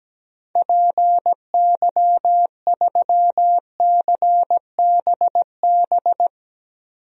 Morse code, heard as a keyed tone, about 17 wpm, 700 Hz. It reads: PY3CBB
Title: DXCC-100 Call Signs - 6 Characters - 17wpm